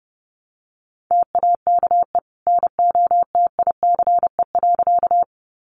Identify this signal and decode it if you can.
Morse code, heard as a keyed tone, about 30 wpm, 700 Hz. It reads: TAKE NOTICE.
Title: Sentences from Top 400 Words 30wpm